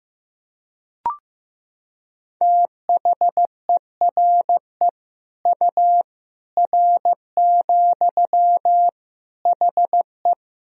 Morse code, heard as a keyed tone, about 15 wpm, 700 Hz. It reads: THERE U R, HE
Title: Sentences from Top 100 Words and Abbreviations 15wpm